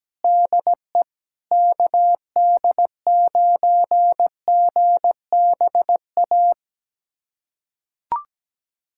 Morse code, heard as a keyed tone, about 17 wpm, 700 Hz. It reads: DE KD9GBA
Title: USA Callsign DE Callsign 17wpm